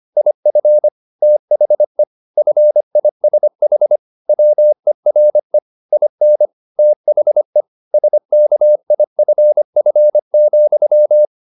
Morse code, heard as a keyed tone, about 25 wpm, 600 Hz. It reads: IF THE FISH WERE IN THE SKIFF,